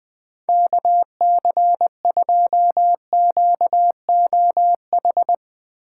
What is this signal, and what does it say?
Morse code, heard as a keyed tone, about 20 wpm, 700 Hz. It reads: KC2QOH
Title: Speed Racing - USA 2x3 Call Signs 17wpm